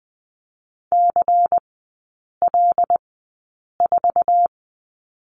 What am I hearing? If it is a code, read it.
Morse code, heard as a keyed tone, about 20 wpm, spaced as for 8 wpm, 700 Hz. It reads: CL4
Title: 133 Lesson 22e - Question Mark - Three Characters 20wpm